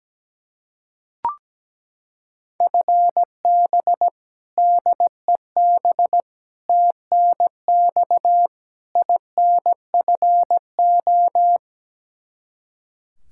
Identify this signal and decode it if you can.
Morse code, heard as a keyed tone, about 17 wpm, 700 Hz. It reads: FB DEB TNX INFO
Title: K1USN SST QSO Practice 17wpm